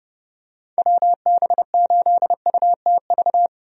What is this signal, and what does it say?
Morse code, heard as a keyed tone, about 30 wpm, 700 Hz. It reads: WB8UTV
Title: USA Call Signs - 6 Characters - 30wpm